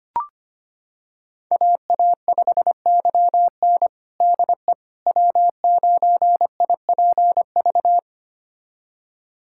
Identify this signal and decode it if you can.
Morse code, heard as a keyed tone, about 25 wpm, 700 Hz. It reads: AA5YN DE W9IPV
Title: USA Callsign DE Callsign - 5 Characters - 25wpm